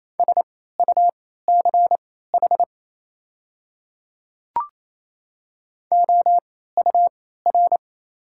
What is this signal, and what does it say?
Morse code, heard as a keyed tone, about 28 wpm, spaced as for 16 wpm, 690 Hz. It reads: SUCH OUR
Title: 3x Character Spacing - Top 300 Words - 28wpm